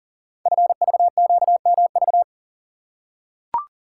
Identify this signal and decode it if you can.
Morse code, heard as a keyed tone, about 38 wpm, 700 Hz. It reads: FVQMV
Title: Code Groups - 5 Letter:Numbers - 40wpm